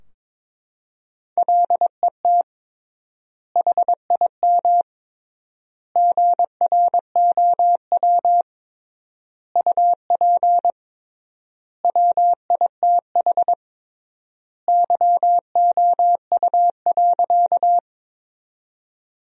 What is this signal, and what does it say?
Morse code, heard as a keyed tone, about 22 wpm, 700 Hz. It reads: LET HIM GROW UP WITH YOU.
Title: 3x Word-Spacing - Sentences from Top 300 Words - 22wpm